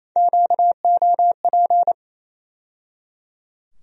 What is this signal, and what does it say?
Morse code, heard as a keyed tone, about 28 wpm, 700 Hz. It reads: QOP
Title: USA Call Sign Postfixes - 3 Letters 28wpm